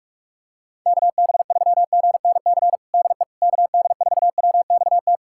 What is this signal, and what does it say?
Morse code, heard as a keyed tone, about 43 wpm, 690 Hz. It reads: KB3GNC DE KD4WXT
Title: USA Callsign DE Callsign - 6 Characters - 45wpm